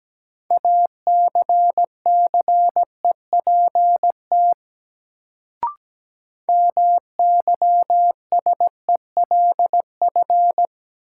Morse code, heard as a keyed tone, about 17 wpm, 700 Hz. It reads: ACCEPT MYSELF
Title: Sentences from Top 1300 Words - New Word Drill - 17wpm